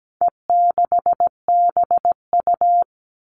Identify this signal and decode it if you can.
Morse code, heard as a keyed tone, about 17 wpm, 700 Hz. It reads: E6BU
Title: All Cut Numbers - 4-Digit 17wpm